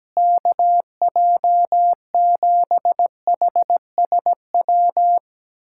Morse code, heard as a keyed tone, about 17 wpm, 700 Hz. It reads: KJ7HSW
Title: USA Callsign DE Callsign - 6 Characters - 17wpm